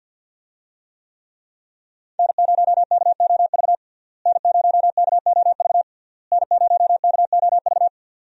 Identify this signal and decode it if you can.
Morse code, heard as a keyed tone, about 49 wpm, 700 Hz. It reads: N0KOV N0KOV N0KOV
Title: USA Call Signs - 5 Characters - Repeated 3x - 50wpm